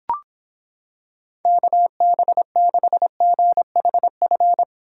Morse code, heard as a keyed tone, about 26 wpm, 700 Hz. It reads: KB6GHF